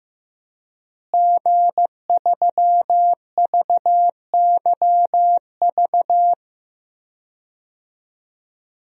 Morse code, heard as a keyed tone, about 15 wpm, 700 Hz. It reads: G3VYV
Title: DXCC-100 Call Signs - 5 Characters - Repeated 3x - 15wpm